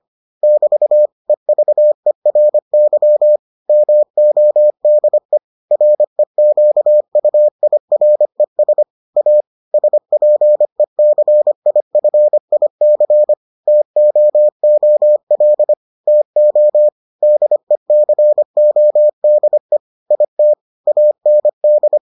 Morse code, heard as a keyed tone, about 25 wpm, 600 Hz. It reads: <BT> EVERY MODE REQUIRES A SPECIFIC TOOL TO DECODE IT AND